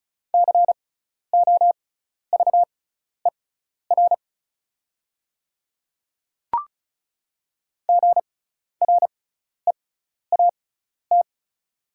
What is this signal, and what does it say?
Morse code, heard as a keyed tone, about 36 wpm, spaced as for 12 wpm, 690 Hz. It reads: COVER GREAT
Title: 6x Character Spacing - Top 300 Words - 35wpm